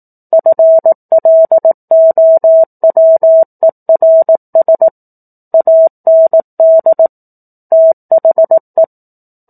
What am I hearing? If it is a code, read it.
Morse code, heard as a keyed tone, about 18 wpm, 650 Hz. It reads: FLOWERS AND THE
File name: 04-Chapter-04 18 WPM